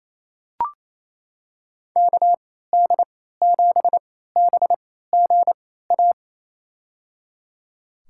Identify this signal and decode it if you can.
Morse code, heard as a keyed tone, about 28 wpm, spaced as for 16 wpm, 700 Hz. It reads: KD7BGA
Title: USA Call Signs - 6 Characters - 3x Character Spacing - 28wpm